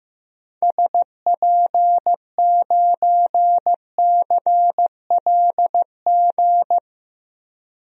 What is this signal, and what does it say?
Morse code, heard as a keyed tone, about 15 wpm, 700 Hz. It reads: SP9CLG
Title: DXCC-100 Call Signs - 6 Characters - 15wpm